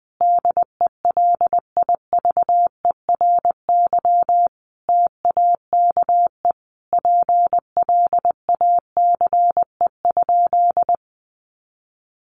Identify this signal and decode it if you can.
Morse code, heard as a keyed tone, about 20 wpm, 700 Hz. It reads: DELIVERY TAKE PLACE?
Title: Sentences from Top 1500 Words 20wpm